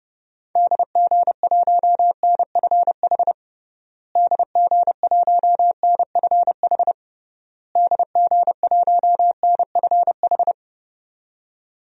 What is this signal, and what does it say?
Morse code, heard as a keyed tone, about 30 wpm, 700 Hz. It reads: DG1NFH DG1NFH DG1NFH
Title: DXCC-100 Call Signs - 6 Characters - Repeated 3x - 30wpm